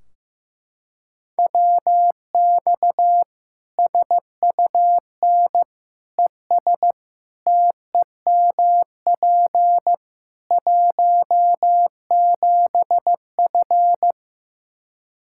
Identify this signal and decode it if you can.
Morse code, heard as a keyed tone, about 14 wpm, 700 Hz. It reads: WX SUN ES TEMP 17F
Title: K1USN SST QSO Practice 15wpm